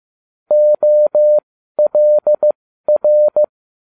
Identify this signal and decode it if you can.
Morse code, heard as a keyed tone, about 15 wpm, spaced as for 12 wpm, 600 Hz. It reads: OLR
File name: cw / ubacw20 3